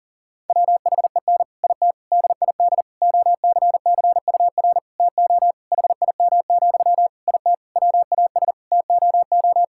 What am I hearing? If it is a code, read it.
Morse code, heard as a keyed tone, about 37 wpm, 700 Hz. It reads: WHEN IT DID OCCUR TO HIM, IT WAS TOO
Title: Sentences from Top 1000 Words - Review Part 6 - 40wpm